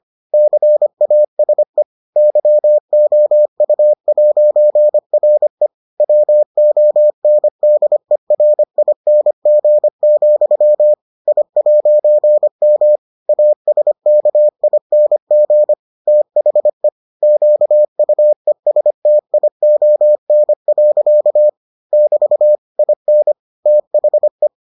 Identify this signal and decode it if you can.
Morse code, heard as a keyed tone, about 25 wpm, 600 Hz. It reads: CASE YOU'RE WONDERING, I'M ASKING THE QUESTION. <BT> IN THE